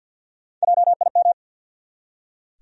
Morse code, heard as a keyed tone, about 46 wpm, 700 Hz. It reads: JIM
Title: USA Common Names - Max 4 Letters - 50wpm